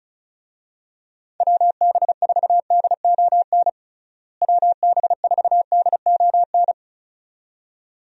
Morse code, heard as a keyed tone, about 36 wpm, 700 Hz. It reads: WB4DON WB4DON
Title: USA Call Signs - 6 Characters - Repeated 2x - 35wpm